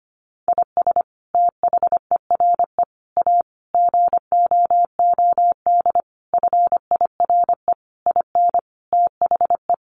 Morse code, heard as a keyed tone, about 25 wpm, 700 Hz. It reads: IS THERE A GOOD FIRE IN THE
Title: Sentences from Top 1400 Words 25wpm